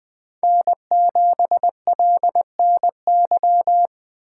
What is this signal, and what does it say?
Morse code, heard as a keyed tone, about 20 wpm, 700 Hz. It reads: N7LNY
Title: Speed Racing - USA 1x3 Call Signs 17wpm